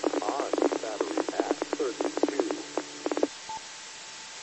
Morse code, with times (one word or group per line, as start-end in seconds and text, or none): none